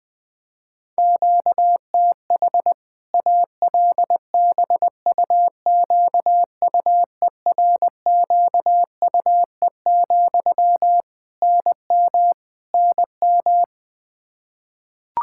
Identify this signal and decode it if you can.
Morse code, heard as a keyed tone, about 20 wpm, 700 Hz. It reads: QTH ALBUQUERQUE, NM NM
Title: Practice QSO - Exchange 1 - Beginning - 20wpm